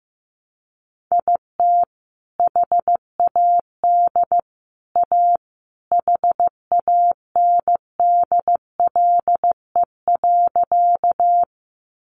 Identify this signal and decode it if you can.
Morse code, heard as a keyed tone, about 15 wpm, 700 Hz. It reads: IT HAD A HANDLE.
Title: Sentences from Top 1100 Words 15wpm